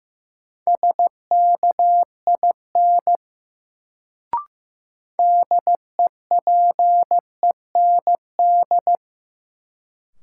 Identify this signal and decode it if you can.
Morse code, heard as a keyed tone, about 16 wpm, 700 Hz. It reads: SKIN DEPEND
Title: Sentences from Top 1300 Words - New Word Drill Alt - 15wpm